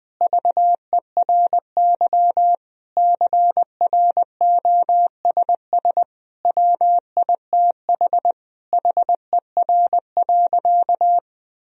Morse code, heard as a keyed tone, about 20 wpm, 700 Hz. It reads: VERY CROSS WITH HER.
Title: Sentences from Top 1000 Words - Review Part 1 - 20wpm